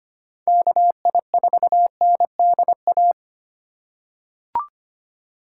Morse code, heard as a keyed tone, about 25 wpm, 700 Hz. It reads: KI4NDA
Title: USA Call Signs - 6 Characters - 4x Character Spacing - 25wpm